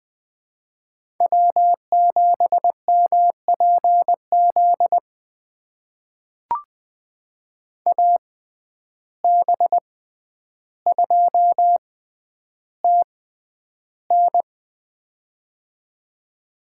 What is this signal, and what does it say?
Morse code, heard as a keyed tone, about 20 wpm, spaced as for 10 wpm, 700 Hz. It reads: W7MPZ AB2TN
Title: USA Call Signs - 5 Characters - 6x Character Spacing - 20wpm